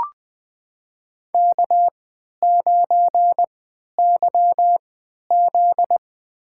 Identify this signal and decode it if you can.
Morse code, heard as a keyed tone, about 20 wpm, spaced as for 11 wpm, 700 Hz. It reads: K9YZ
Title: USA Call Signs - 3x Character Spacing - 20wpm